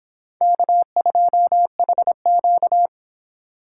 Morse code, heard as a keyed tone, about 26 wpm, 700 Hz. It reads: K2HQ